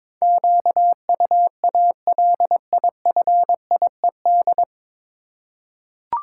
Morse code, heard as a keyed tone, about 22 wpm, 700 Hz. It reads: QUALIFIED
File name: 9 Letter Words - First 3 Letters Given - 22wpm